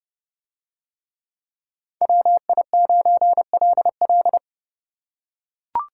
Morse code, heard as a keyed tone, about 29 wpm, 700 Hz. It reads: WI9LL